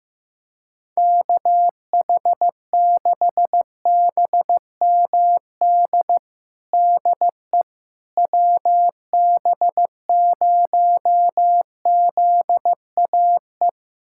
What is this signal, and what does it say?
Morse code, heard as a keyed tone, about 16 wpm, 700 Hz. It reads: KH6BMD DE WB0ZAE
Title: USA Callsign DE Callsign - 6 Characters - 15wpm